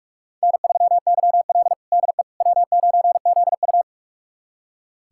Morse code, heard as a keyed tone, about 46 wpm, 700 Hz. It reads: N3YL DE W9ZU
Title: USA Callsign DE Callsign - 4 Characters - 45wpm